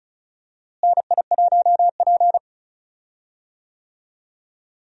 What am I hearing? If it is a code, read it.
Morse code, heard as a keyed tone, about 36 wpm, 700 Hz. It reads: NI1P
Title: USA Call Signs - 4 Characters - Repeated 3x - 35wpm